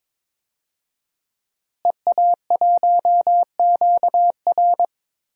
Morse code, heard as a keyed tone, about 22 wpm, 700 Hz. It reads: EA1QR
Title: DXCC-100 Call Signs - 5 Characters - Repeated 2x - 22wpm